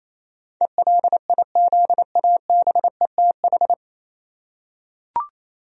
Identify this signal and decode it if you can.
Morse code, heard as a keyed tone, about 28 wpm, 700 Hz. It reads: ELIZABETH
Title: USA Common Names - 28wpm